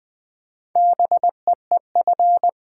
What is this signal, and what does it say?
Morse code, heard as a keyed tone, about 20 wpm, 690 Hz. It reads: BEEF
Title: Difficult Words with Hints - 20wpm